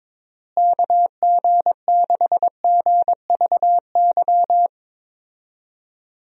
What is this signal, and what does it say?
Morse code, heard as a keyed tone, about 22 wpm, 700 Hz. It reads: KG6GVY
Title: USA Call Signs - 6 Characters - Repeated 3x - 22wpm